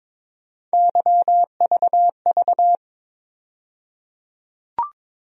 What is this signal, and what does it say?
Morse code, heard as a keyed tone, about 22 wpm, 700 Hz. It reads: YVV
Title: USA Call Sign Postfixes - 3 Letters 22wpm